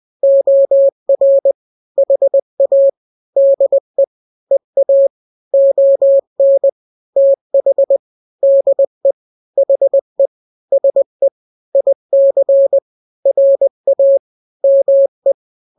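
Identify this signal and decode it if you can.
Morse code, heard as a keyed tone, about 20 wpm, 550 Hz. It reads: OR HA DE EA ON TH DE HE SE IC RA ME
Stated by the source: Morse drill of two-letter n-grams at 20/18 WPM (Farnsworth), from KK7RR.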